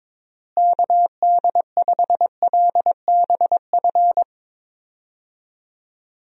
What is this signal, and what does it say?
Morse code, heard as a keyed tone, about 22 wpm, 700 Hz. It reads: KD5LBF
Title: USA Call Signs - 6 Characters - Repeated 3x - 22wpm